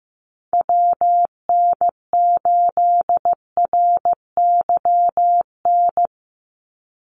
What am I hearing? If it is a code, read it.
Morse code, heard as a keyed tone, about 15 wpm, 700 Hz. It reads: WN8RYN
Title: USA Call Signs 15wpm